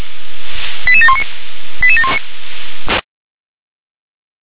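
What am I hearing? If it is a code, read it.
Five-tone selective call (ZVEI), heard as quick runs of five tones, tones of about 70 ms. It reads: 88719 88719
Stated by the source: Five-tone ZVEI selective call received on an SDR.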